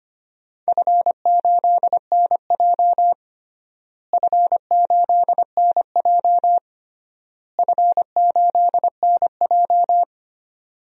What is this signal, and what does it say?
Morse code, heard as a keyed tone, about 25 wpm, 700 Hz. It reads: F8NJ F8NJ F8NJ
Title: DXCC-100 Call Signs - 4 Characters - Repeated 3x - 25wpm